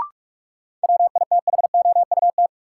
Morse code, heard as a keyed tone, about 46 wpm, 700 Hz. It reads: WITHOUT
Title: Sentences from Top 600 Words - New Word Drill - 45wpm